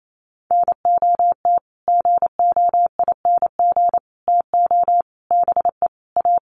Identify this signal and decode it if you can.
Morse code, heard as a keyed tone, about 28 wpm, 700 Hz. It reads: NOT GOING TO BE A
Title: Sets of 5 Words Encore 28wpm